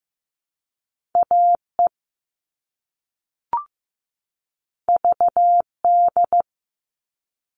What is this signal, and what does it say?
Morse code, heard as a keyed tone, about 15 wpm, 700 Hz. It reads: AE VD
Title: All Cut Numbers - 2-Digit 15wpm